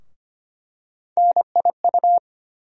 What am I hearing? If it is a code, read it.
Morse code, heard as a keyed tone, about 25 wpm, 700 Hz. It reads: NIU
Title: USA Call Sign Postfixes - 3 Letters 25wpm